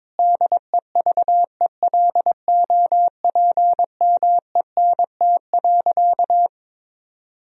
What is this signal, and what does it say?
Morse code, heard as a keyed tone, about 22 wpm, 700 Hz. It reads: DEVELOPMENT.
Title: Sentences from Top 1000 Words - Review Part 4 - 22wpm